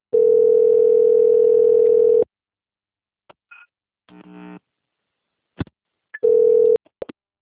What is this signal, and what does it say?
Ringback tone (North America), 440 + 480 Hz.